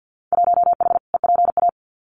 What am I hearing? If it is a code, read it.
Morse code, heard as a keyed tone, about 50 wpm, 700 Hz. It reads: 1H EPA
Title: Winter Field Day Exchanges 50wpm